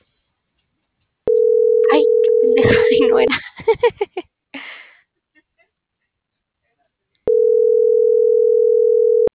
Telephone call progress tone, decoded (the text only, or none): ringback tone